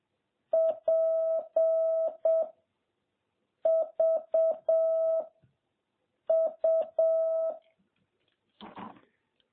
Morse code, heard as a keyed tone, about 7 wpm, 660 Hz. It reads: P V U
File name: jh8cbh / 2021cw